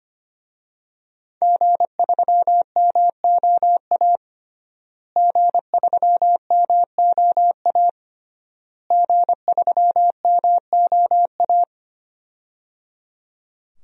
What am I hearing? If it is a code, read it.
Morse code, heard as a keyed tone, about 25 wpm, 700 Hz. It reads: G3MOA G3MOA G3MOA
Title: DXCC-100 Call Signs - 5 Characters - Repeated 3x - 25wpm